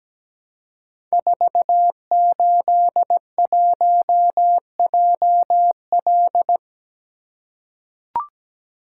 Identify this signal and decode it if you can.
Morse code, heard as a keyed tone, about 17 wpm, 700 Hz. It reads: 481JL
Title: Code Groups - 5 Letter:Numbers - 17wpm